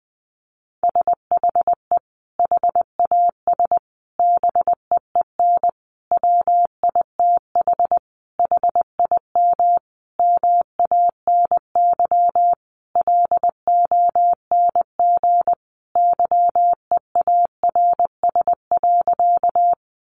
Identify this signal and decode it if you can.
Morse code, heard as a keyed tone, about 20 wpm, 700 Hz. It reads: SHE HAS BEEN WITH HIM MANY LONG YEARS.
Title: Sentences from Top 200 Words 20wpm